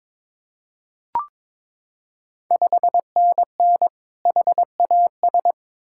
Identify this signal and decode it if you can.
Morse code, heard as a keyed tone, about 22 wpm, 690 Hz. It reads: 5NN HAS
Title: Ontario QSO Party Exchange 22wpm